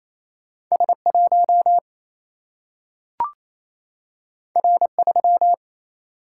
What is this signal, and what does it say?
Morse code, heard as a keyed tone, about 28 wpm, 700 Hz. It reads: S1 R3
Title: Single-Letter Single-Number 28wpm